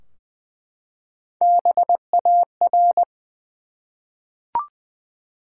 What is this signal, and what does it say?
Morse code, heard as a keyed tone, about 20 wpm, 700 Hz. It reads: BAR
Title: Three Letter Words - 20wpm